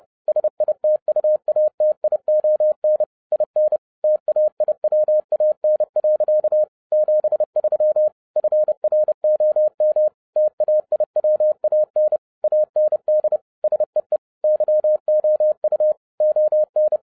Morse code, heard as a keyed tone, about 30 wpm, 610 Hz. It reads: SITUATION IN TAIWAN. 73 FROM TAIWAN AND SEE YOU ON